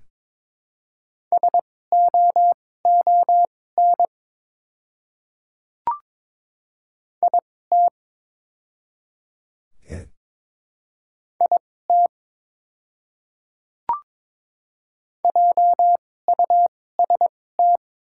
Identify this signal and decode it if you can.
Morse code, heard as a keyed tone, about 22 wpm, spaced as for 16 wpm, 690 Hz. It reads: SOON IT IT JUST
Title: 2x Character Spacing - Top 500 Words - 22wpm